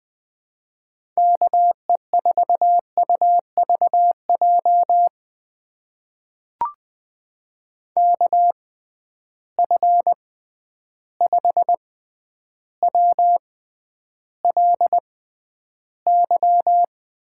Morse code, heard as a keyed tone, about 20 wpm, spaced as for 10 wpm, 700 Hz. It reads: KE4UVJ KF5WLY
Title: USA Call Signs - 6 Characters - 6x Character Spacing - 20wpm